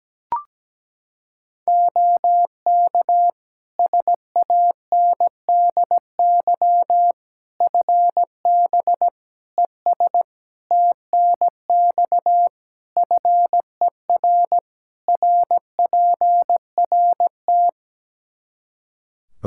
Morse code, heard as a keyed tone, about 17 wpm, 700 Hz. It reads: OK SANDY FB ES TNX FER RPRT
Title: K1USN SST QSO Practice 17wpm